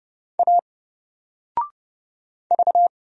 Morse code, heard as a keyed tone, about 30 wpm, 700 Hz. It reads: A V
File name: Single-Letter-Number-30wpm